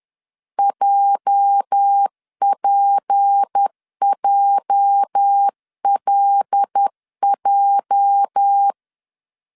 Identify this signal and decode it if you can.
Morse code, heard as a keyed tone, about 10 wpm, 800 Hz. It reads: JPJLJ
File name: LEZIONI / 8 YJ45 / 8 YJ 10wpm 2651